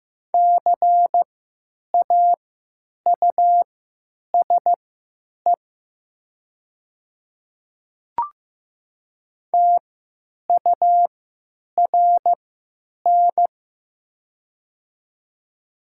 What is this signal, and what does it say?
Morse code, heard as a keyed tone, about 15 wpm, spaced as for 9 wpm, 700 Hz. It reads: CAUSE TURN
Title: 3x Character Spacing - Top 300 Words - 15wpm